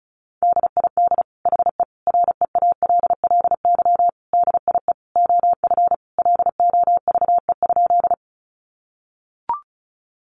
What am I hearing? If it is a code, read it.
Morse code, heard as a keyed tone, about 34 wpm, 700 Hz. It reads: DID HE REALLY DIE OF LOVE?
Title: Sentences from Top 1300 Words 35wpm